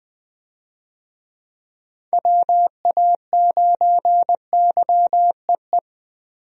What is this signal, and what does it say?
Morse code, heard as a keyed tone, about 20 wpm, 700 Hz. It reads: WA9YEE
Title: USA Call Signs - 6 Characters - Repeated 3x - 20wpm